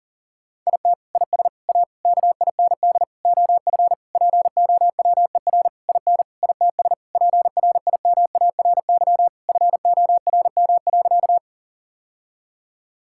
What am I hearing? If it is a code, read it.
Morse code, heard as a keyed tone, about 40 wpm, 700 Hz. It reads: IT IS A KIND OF POWER IN ITS PRIMARY FORM.
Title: Sentences from Top 1000 Words - Review Part 5 - 40wpm